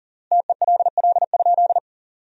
Morse code, heard as a keyed tone, about 40 wpm, 700 Hz. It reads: TELL?
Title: 3x Word-Spacing - Sentences from Top 300 Words - 40wpm